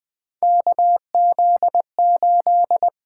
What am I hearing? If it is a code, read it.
Morse code, heard as a keyed tone, about 20 wpm, 700 Hz. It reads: KZ8
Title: USA Call Signs - 8x Character Spacing - 20wpm